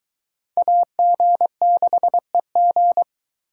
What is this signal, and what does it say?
Morse code, heard as a keyed tone, about 22 wpm, 700 Hz. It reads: AG6EG